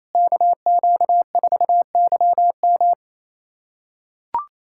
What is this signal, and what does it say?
Morse code, heard as a keyed tone, about 28 wpm, 700 Hz. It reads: KQ4YM
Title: USA Callsign DE Callsign 28wpm